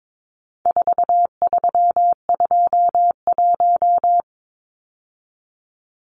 Morse code, heard as a keyed tone, about 22 wpm, 700 Hz. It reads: 4321
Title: Numbers - 4-Digit 22wpm